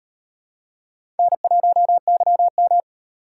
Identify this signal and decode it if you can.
Morse code, heard as a keyed tone, about 38 wpm, 700 Hz. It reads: N1YM